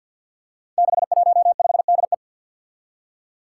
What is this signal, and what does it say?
Morse code, heard as a keyed tone, about 46 wpm, 700 Hz. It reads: 615DE
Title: Five Challenging Letter-Number Combinations 50wpm